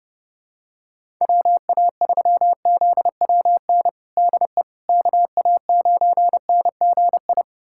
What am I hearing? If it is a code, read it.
Morse code, heard as a keyed tone, about 30 wpm, 700 Hz. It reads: WA3ZWN DE KA9NGI
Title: USA Callsign DE Callsign 30wpm